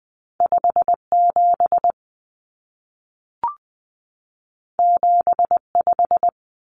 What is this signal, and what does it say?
Morse code, heard as a keyed tone, about 20 wpm, 700 Hz. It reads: 57 75
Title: Common Cut Numbers - 2-Digit 20wpm